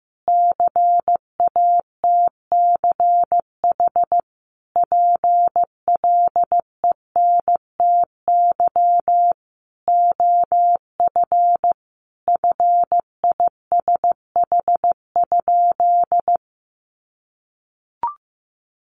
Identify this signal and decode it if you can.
Morse code, heard as a keyed tone, about 15 wpm, 700 Hz. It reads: CATCH PLENTY OF FISH?
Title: Sentences from Top 1500 Words 15wpm